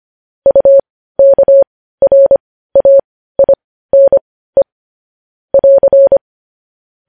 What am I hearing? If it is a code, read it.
Morse code, heard as a keyed tone, about 25 wpm, spaced as for 15 wpm, 550 Hz. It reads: UKRAINE <AR>
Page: news headlines in morse code at 15 wpm